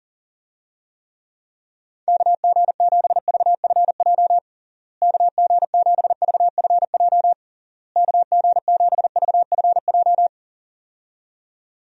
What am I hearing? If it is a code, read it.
Morse code, heard as a keyed tone, about 40 wpm, 700 Hz. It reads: KG7VFJ KG7VFJ KG7VFJ
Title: USA Call Signs - 6 Characters - Repeated 3x - 40wpm